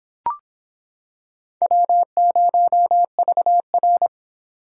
Morse code, heard as a keyed tone, about 26 wpm, 700 Hz. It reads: W0VR